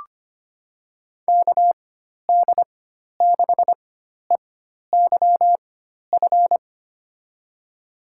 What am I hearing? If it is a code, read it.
Morse code, heard as a keyed tone, about 25 wpm, spaced as for 12 wpm, 700 Hz. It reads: KD6EYF
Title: USA Call Signs - 4x Character Spacing - 25wpm